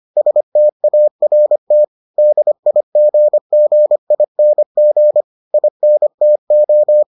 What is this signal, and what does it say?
Morse code, heard as a keyed tone, about 25 wpm, 600 Hz. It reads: START DIGGING INTO